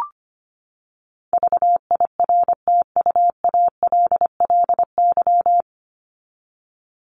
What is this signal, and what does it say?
Morse code, heard as a keyed tone, about 25 wpm, 700 Hz. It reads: VIRTUALLY
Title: Top 1501-1750 Words 25wpm